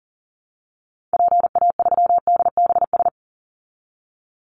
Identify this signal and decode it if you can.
Morse code, heard as a keyed tone, about 40 wpm, 700 Hz. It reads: PA3DBS
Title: Call Signs Repeated 3x - 40wpm